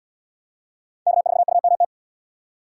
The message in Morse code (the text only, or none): K4UMT